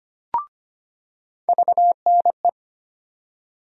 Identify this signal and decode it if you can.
Morse code, heard as a keyed tone, about 25 wpm, 700 Hz. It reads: VNE
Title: USA Call Sign Postfixes - 3 Letters 25wpm